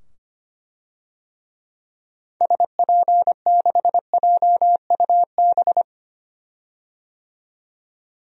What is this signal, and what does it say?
Morse code, heard as a keyed tone, about 25 wpm, 700 Hz. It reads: SP6JUB
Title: DXCC-100 Call Signs - 6 Characters - 25wpm